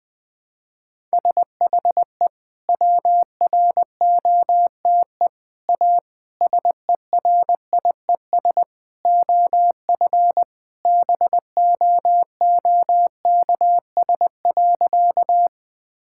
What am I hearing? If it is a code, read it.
Morse code, heard as a keyed tone, about 20 wpm, 700 Hz. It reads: SHE WROTE A SERIES OF BOOKS.
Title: Sentences with 3-Word Phrases - Spoken Beginning - 20wpm